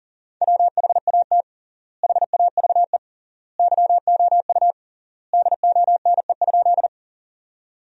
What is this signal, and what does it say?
Morse code, heard as a keyed tone, about 40 wpm, 700 Hz. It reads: WHAT HAVE YOU DONE?
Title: 3x Word-Spacing - Sentences from Top 500 Words - 40wpm